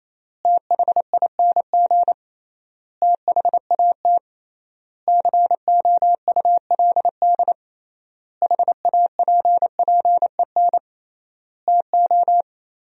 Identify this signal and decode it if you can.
Morse code, heard as a keyed tone, about 28 wpm, 700 Hz. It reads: THING THAT COULD HAPPEN TO
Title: 3x Word-Spacing - Sentences from Top 400 Words - 28wpm